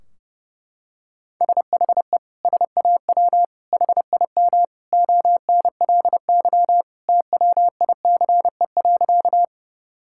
Morse code, heard as a keyed tone, about 32 wpm, 700 Hz. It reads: SHE SAW HIM ONLY TWICE.
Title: Sentences from Top 1600 Words 30wpm